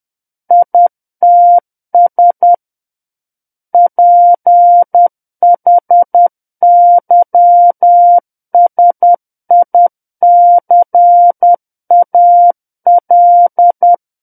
Morse code, heard as a keyed tone, about 10 wpm, 700 Hz. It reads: ITS PHYSICAL
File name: Book One / 01-Chapter 1 - 10 WPM